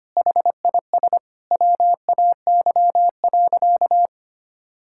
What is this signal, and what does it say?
Morse code, heard as a keyed tone, about 25 wpm, 700 Hz. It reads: HIS WAY.
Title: Sentences from Top 1000 Words - Review Part 6 - 25wpm